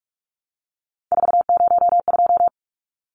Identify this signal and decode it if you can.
Morse code, heard as a keyed tone, about 44 wpm, 700 Hz. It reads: 402
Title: Numbers - 3-Digit 45wpm